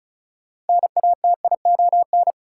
Morse code, heard as a keyed tone, about 34 wpm, 700 Hz. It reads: NATION
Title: Sentences from Top 900 Words - New Word Drill - 35wpm